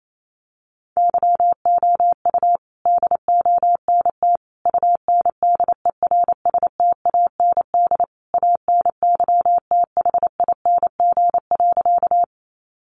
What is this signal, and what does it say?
Morse code, heard as a keyed tone, about 28 wpm, 700 Hz. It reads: YOU DONT UNDERSTAND ANYTHING.
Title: Sentences from Top 1000 Words 28wpm